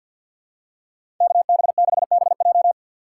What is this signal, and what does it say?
Morse code, heard as a keyed tone, about 53 wpm, 700 Hz. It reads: KB6BJ
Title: USA Call Signs - 5 Characters - 3x Character Spacing - 50wpm